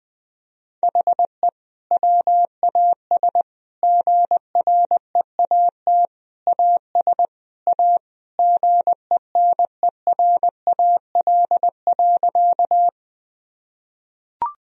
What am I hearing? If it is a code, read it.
Morse code, heard as a keyed tone, about 21 wpm, 700 Hz. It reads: HE WAS GREAT AS A GENERAL.
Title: Sentences from Top 1000 Words - Review Part 3 - 20wpm